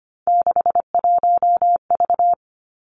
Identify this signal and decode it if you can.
Morse code, heard as a keyed tone, about 25 wpm, 700 Hz. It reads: 61V
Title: Three Challenging Letter-Number Combinations 25wpm